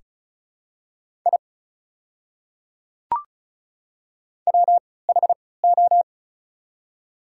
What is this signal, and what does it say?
Morse code, heard as a keyed tone, about 33 wpm, spaced as for 19 wpm, 690 Hz. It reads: I WHO
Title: 3x Character Spacing - Top 100 Words - 35wpm